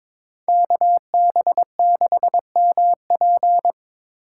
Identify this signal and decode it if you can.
Morse code, heard as a keyed tone, about 26 wpm, 700 Hz. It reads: KB6MP
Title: Speed Racing - USA 2x2 Call Signs 22wpm